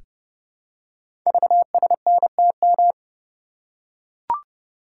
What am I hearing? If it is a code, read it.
Morse code, heard as a keyed tone, about 30 wpm, 700 Hz. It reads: VSNTM
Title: Code Groups - 5 Letter:Numbers - 30wpm